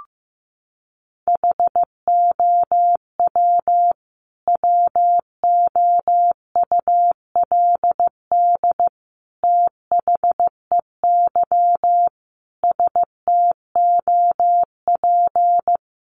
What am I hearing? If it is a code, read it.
Morse code, heard as a keyed tone, about 15 wpm, 700 Hz. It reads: HOW WOULD THEY STOP
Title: Sentences from Top 300 Words 15wpm